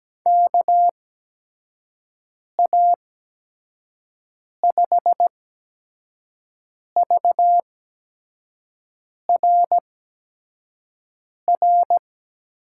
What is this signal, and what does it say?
Morse code, heard as a keyed tone, about 17 wpm, spaced as for 5 wpm, 700 Hz. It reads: KA5VRR
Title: USA Call Signs - 6 Characters - 8x Character Spacing - 17wpm